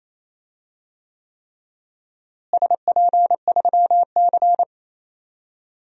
Morse code, heard as a keyed tone, about 28 wpm, 700 Hz. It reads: SP3C
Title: DXCC-100 Call Signs - 4 Characters - Repeated 3x - 28wpm